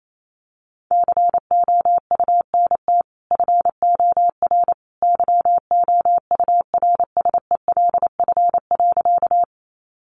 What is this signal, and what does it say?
Morse code, heard as a keyed tone, about 28 wpm, 700 Hz. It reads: COUNT FOR YOURSELF.
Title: Sentences from Top 1500 Words 28wpm